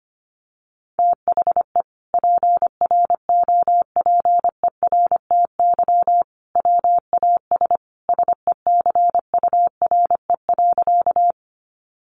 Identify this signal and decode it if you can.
Morse code, heard as a keyed tone, about 25 wpm, 700 Hz. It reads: THE PROPERTY WAS SECURE.
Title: Sentences from Top 1200 Words 25wpm